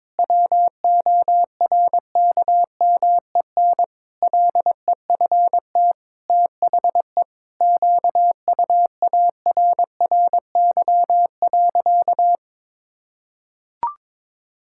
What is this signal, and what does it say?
Morse code, heard as a keyed tone, about 22 wpm, 700 Hz. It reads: WORKMEN LEFT THE QUARRY.
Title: Pangram Challenge v2 - 22wpm